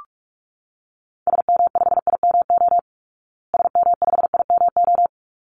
Morse code, heard as a keyed tone, about 44 wpm, 700 Hz. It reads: SM5IMO SM5IMO
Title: Call Signs Repeated 3x - 45wpm